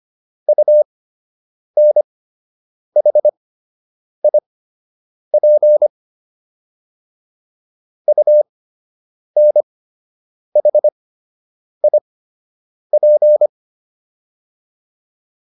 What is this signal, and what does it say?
Morse code, heard as a keyed tone, about 25 wpm, spaced as for 8 wpm, 600 Hz. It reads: UNHIP UNHIP